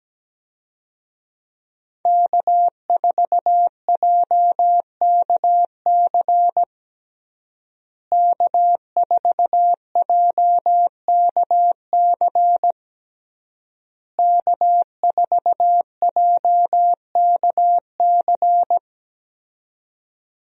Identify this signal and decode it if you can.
Morse code, heard as a keyed tone, about 17 wpm, 700 Hz. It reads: K4JKC K4JKC K4JKC
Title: USA Call Signs - 5 Characters - Repeated 3x - 17wpm